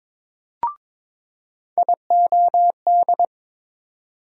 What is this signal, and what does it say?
Morse code, heard as a keyed tone, about 22 wpm, 700 Hz. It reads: IOD
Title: USA Call Sign Postfixes - 3 Letters 22wpm